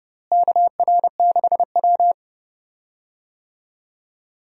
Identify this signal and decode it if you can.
Morse code, heard as a keyed tone, about 30 wpm, 700 Hz. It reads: KR6W
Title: USA Call Signs - 4 Characters - Repeated 2x - 30wpm